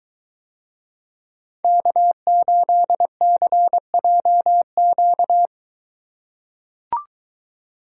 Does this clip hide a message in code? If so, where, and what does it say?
Yes: Morse code from 1.6 to 5.5 s, heard as a keyed tone, about 23 wpm, 700 Hz. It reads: K8CJQ